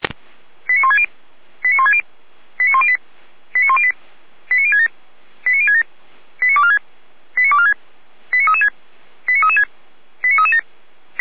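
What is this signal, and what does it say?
Five-tone selective call (ZVEI), heard as quick runs of five tones, tones of about 70 ms. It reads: 89160 89160 89108 89108 89067 89067 89246 89246 89336 89336 89337
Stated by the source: ZVEI five-tone call received by SDR.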